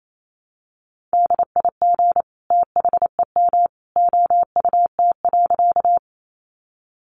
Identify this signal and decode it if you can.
Morse code, heard as a keyed tone, about 28 wpm, 700 Hz. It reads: DIG THEM OUT.
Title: A Sound Of Thunder 28wpm